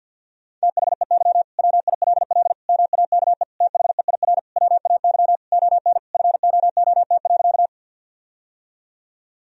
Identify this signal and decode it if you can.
Morse code, heard as a keyed tone, about 50 wpm, 700 Hz. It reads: THEY WILL MAKE THEIR WAY ON FOOT.
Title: Sentences from Top 1000 Words - Review Part 3 - 50wpm